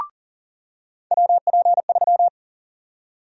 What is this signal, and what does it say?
Morse code, heard as a keyed tone, about 40 wpm, 700 Hz. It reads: WP3
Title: USA Call Sign Prefixes - 40wpm